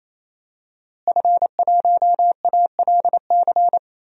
Morse code, heard as a keyed tone, about 28 wpm, 690 Hz. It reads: F1ALC
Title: DXCC-100 Call Signs - 5 Characters - Repeated 3x - 28wpm